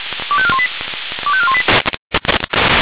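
Five-tone selective call (ZVEI), heard as quick runs of five tones, tones of about 70 ms. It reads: 25418 25418
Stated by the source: ZVEI five-tone selective call received on an SDR.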